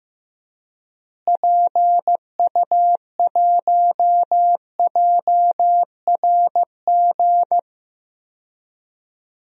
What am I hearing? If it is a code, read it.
Morse code, heard as a keyed tone, about 15 wpm, 700 Hz. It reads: PU1JRG
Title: DXCC-100 Call Signs - 6 Characters - Repeated 2x - 15wpm